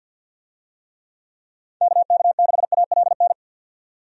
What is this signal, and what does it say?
Morse code, heard as a keyed tone, about 49 wpm, 700 Hz. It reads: KK6ALN
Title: USA Call Signs - 6 Characters - 50wpm